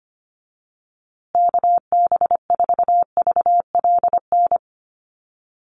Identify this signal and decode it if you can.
Morse code, heard as a keyed tone, about 25 wpm, 700 Hz. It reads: KB4VLN
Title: USA Call Signs Repeated 2x 25wpm